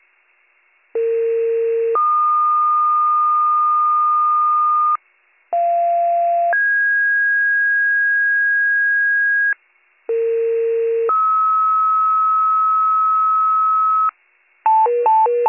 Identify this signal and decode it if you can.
Two-tone page, heard as a one-second tone, then a three-second tone, about 460 then 1220 Hz, about 690 then 1650 Hz, about 460 then 1290 Hz. It is heard from 0.9 to 14.1 s.